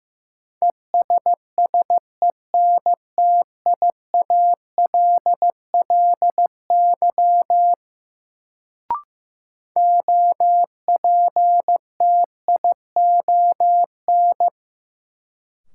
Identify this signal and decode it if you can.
Morse code, heard as a keyed tone, about 15 wpm, 700 Hz. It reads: ESSENTIALLY OPTION
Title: Sentences from Top 1300 Words - New Word Drill Alt - 15wpm